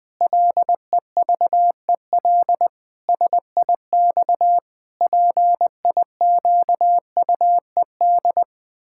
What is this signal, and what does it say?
Morse code, heard as a keyed tone, about 20 wpm, 700 Hz. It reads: LEVEL SIX PIQUED
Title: Pangram Challenge v2 - 20wpm